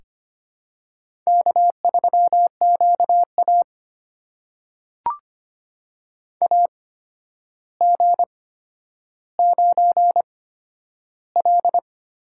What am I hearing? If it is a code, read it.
Morse code, heard as a keyed tone, about 25 wpm, spaced as for 11 wpm, 700 Hz. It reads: K3QA AG9L
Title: USA Call Signs - 8x Character Spacing - 25wpm